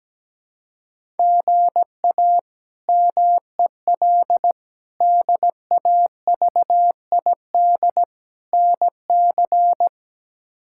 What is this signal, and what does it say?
Morse code, heard as a keyed tone, about 17 wpm, 700 Hz. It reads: GA MEL DAVID NC
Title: K1USN SST QSO Practice - 17wpm